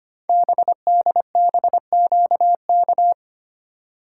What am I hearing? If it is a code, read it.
Morse code, heard as a keyed tone, about 25 wpm, 700 Hz. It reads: BDBQK
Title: Code Groups - 5 Letters - 25wpm